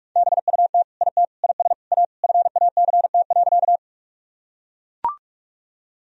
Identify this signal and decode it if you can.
Morse code, heard as a keyed tone, about 41 wpm, 700 Hz. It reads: BUT IT IS A FACT.
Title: Sentences from Top 1000 Words - Review Part 3 - 45wpm